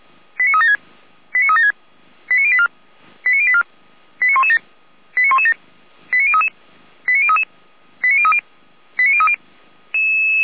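Five-tone selective call (ZVEI), heard as quick runs of five tones, tones of about 70 ms. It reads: 89376 89376 89084 89084 89117 89117 89033 89033 89030 89030
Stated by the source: ZVEI five-tone call 89080 received via SDR.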